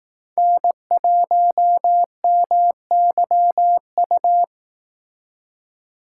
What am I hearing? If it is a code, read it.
Morse code, heard as a keyed tone, about 18 wpm, 700 Hz. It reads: N1MYU
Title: Speed Racing - USA 1x3 Call Signs 15wpm